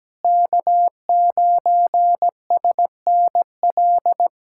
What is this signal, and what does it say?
Morse code, heard as a keyed tone, about 17 wpm, 700 Hz. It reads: K9SNL
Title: USA Callsign DE Callsign - 5 Characters - 17wpm